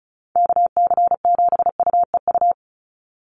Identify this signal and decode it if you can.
Morse code, heard as a keyed tone, about 36 wpm, 700 Hz. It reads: KC7UEU
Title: USA Call Signs 35wpm